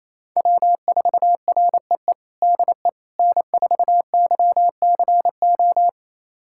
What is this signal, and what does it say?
Morse code, heard as a keyed tone, about 28 wpm, 700 Hz. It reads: W4REE DE N4YCO
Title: USA Callsign DE Callsign - 5 Characters - 28wpm